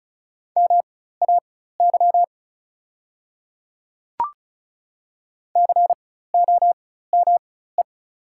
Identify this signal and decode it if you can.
Morse code, heard as a keyed tone, about 36 wpm, spaced as for 16 wpm, 690 Hz. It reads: MAY COME
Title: 4x Character Spacing - Top 100 Words - 35wpm